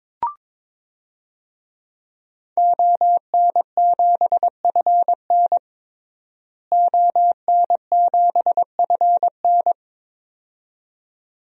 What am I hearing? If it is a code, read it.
Morse code, heard as a keyed tone, about 21 wpm, 700 Hz. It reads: ON7FN ON7FN
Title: DXCC-100 Call Signs - 5 Characters - Repeated 2x - 22wpm